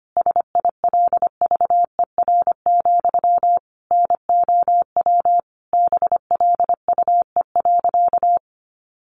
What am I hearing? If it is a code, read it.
Morse code, heard as a keyed tone, about 25 wpm, 700 Hz. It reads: SILVER, NOW BLUE.
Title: A Sound Of Thunder 25wpm